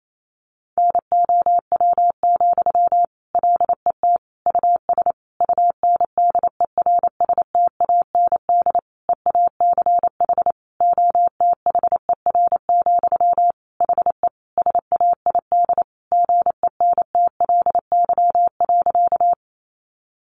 Morse code, heard as a keyed tone, about 28 wpm, 700 Hz. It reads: NOW, LET US UNDERSTAND EACH OTHER, HE SAID GENTLY.
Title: Sentences from Top 1500 Words 28wpm